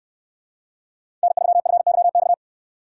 Morse code, heard as a keyed tone, about 63 wpm, 700 Hz. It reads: N3FYX